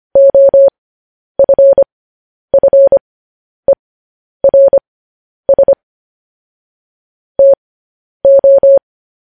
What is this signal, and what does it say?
Morse code, heard as a keyed tone, about 25 wpm, spaced as for 10 wpm, 550 Hz. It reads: OFFERS TO
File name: podcast / News headlines-1x-25x10